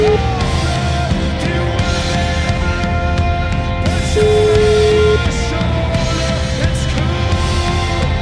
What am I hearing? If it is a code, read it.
Ringback tone (Europe, 425 Hz), with noise.